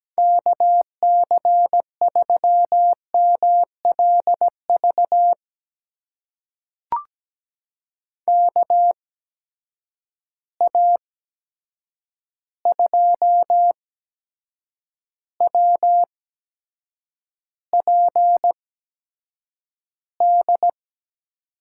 Morse code, heard as a keyed tone, about 17 wpm, spaced as for 7 wpm, 700 Hz. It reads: KC3MLV KA2WPD
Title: USA Call Signs - 6 Characters - 8x Character Spacing - 17wpm